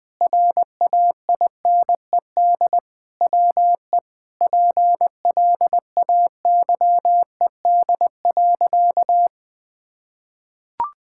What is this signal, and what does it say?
Morse code, heard as a keyed tone, about 20 wpm, 700 Hz. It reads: RAINED WE PLAYED.
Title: Sentences with 6-Word Phrases - 20wpm